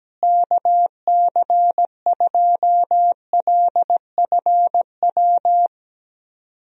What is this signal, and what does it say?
Morse code, heard as a keyed tone, about 17 wpm, 700 Hz. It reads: KC2LFW
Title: USA Call Signs - 6 Characters - Repeated 2x - 17wpm